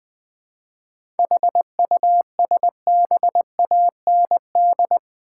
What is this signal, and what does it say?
Morse code, heard as a keyed tone, about 20 wpm, 700 Hz. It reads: HUSBAND
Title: Seven Letter Words - 20wpm